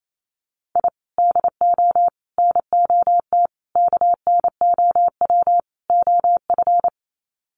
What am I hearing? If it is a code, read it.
Morse code, heard as a keyed tone, about 28 wpm, 700 Hz. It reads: I DO NOT KNOW OF
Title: Sets of 5 Words Encore2 28wpm